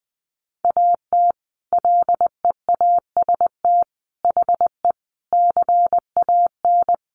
Morse code, heard as a keyed tone, about 20 wpm, 700 Hz. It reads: AT LEAST HE CAN
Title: Sentences from Top 900 Words 20wpm